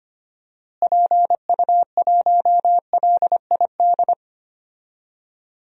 Morse code, heard as a keyed tone, about 25 wpm, 690 Hz. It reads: PU1LID
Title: DXCC-100 Call Signs - 6 Characters - Repeated 3x - 25wpm